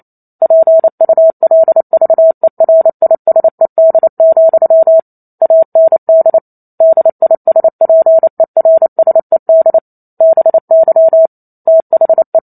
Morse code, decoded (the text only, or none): PULVERISED, AND DISPERSED BY THE